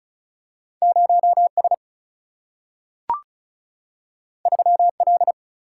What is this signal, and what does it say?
Morse code, heard as a keyed tone, about 36 wpm, 700 Hz. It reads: 0 S 3 L
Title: Single-Letter Single-Number 35wpm